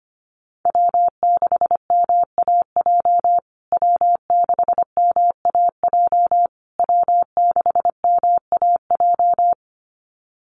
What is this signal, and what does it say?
Morse code, heard as a keyed tone, about 25 wpm, 700 Hz. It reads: W6MAJ W6MAJ W6MAJ
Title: USA Call Signs Repeated 3x 25wpm